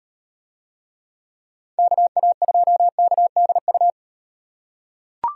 Morse code, heard as a keyed tone, about 35 wpm, 700 Hz. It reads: KA2KDU